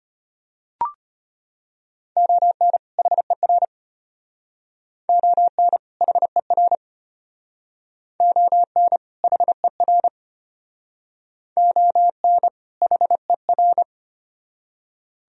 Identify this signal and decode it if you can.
Morse code, heard as a keyed tone, about 32 wpm, 700 Hz. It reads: ON HER ON HER ON HER ON HER